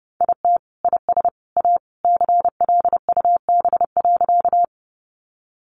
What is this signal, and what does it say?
Morse code, heard as a keyed tone, about 30 wpm, 700 Hz. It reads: IT IS A CLUB.
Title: Sentences from Top 1200 Words 30wpm